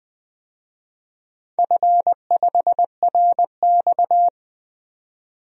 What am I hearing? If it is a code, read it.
Morse code, heard as a keyed tone, about 20 wpm, 700 Hz. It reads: F5RX
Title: DXCC-100 Call Signs - 4 Characters - 20wpm